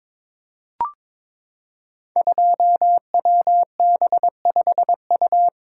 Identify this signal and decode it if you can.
Morse code, heard as a keyed tone, about 22 wpm, 700 Hz. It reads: 2WB5U
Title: Code Groups - 5 Letter:Numbers - 22wpm